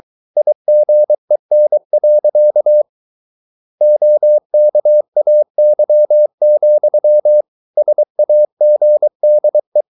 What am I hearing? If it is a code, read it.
Morse code, heard as a keyed tone, about 23 wpm, 600 Hz. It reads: IGEN. OKAY, SAGDE
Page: Ferie 23 wpm